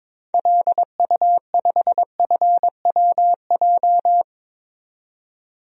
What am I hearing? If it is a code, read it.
Morse code, heard as a keyed tone, about 21 wpm, 700 Hz. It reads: LU5FWJ
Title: DXCC-100 Call Signs - 6 Characters - 22wpm